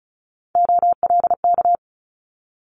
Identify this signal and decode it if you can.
Morse code, heard as a keyed tone, about 36 wpm, 700 Hz. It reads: OLK
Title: Multi-letter Phonemes Quick Fire 35wpm